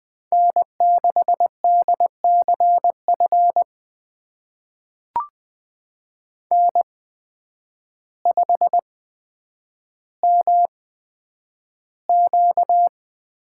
Morse code, heard as a keyed tone, about 20 wpm, spaced as for 9 wpm, 700 Hz. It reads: N6DCF N5MQ
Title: USA Call Signs - 5 Characters - 8x Character Spacing - 20wpm